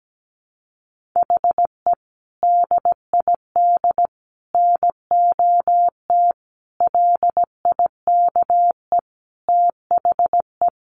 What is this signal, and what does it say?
Morse code, heard as a keyed tone, about 17 wpm, 700 Hz. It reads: HE DID NOT LIKE THE
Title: Sentences from Top 1200 Words 17wpm